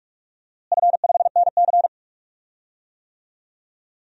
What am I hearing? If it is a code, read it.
Morse code, heard as a keyed tone, about 43 wpm, 700 Hz. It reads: F5NC
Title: DXCC-100 Call Signs - 4 Characters - Repeated 3x - 45wpm